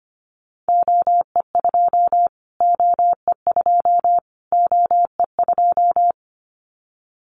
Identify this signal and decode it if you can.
Morse code, heard as a keyed tone, about 25 wpm, 700 Hz. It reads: OE2 OE2 OE2
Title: Callsign Prefixes for Top 25 DXCC - Repeated 3x - 25wpm